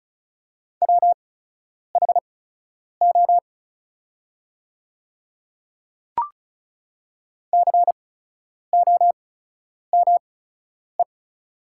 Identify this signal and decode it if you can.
Morse code, heard as a keyed tone, about 34 wpm, spaced as for 10 wpm, 690 Hz. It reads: WHO COME
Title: 8x Character Spacing - Top 100 Words - 35wpm